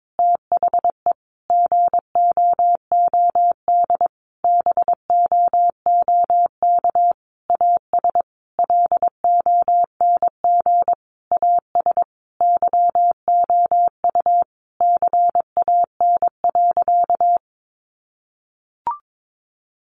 Morse code, heard as a keyed tone, about 22 wpm, 700 Hz. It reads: THE GOOD BOOK AS LONG AS YOU CAN.
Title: Sentences from Top 1200 Words 22wpm